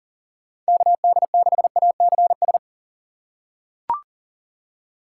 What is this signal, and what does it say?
Morse code, heard as a keyed tone, about 37 wpm, 700 Hz. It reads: KD6ACS
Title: USA Call Signs - 6 Characters - 3x Character Spacing - 40wpm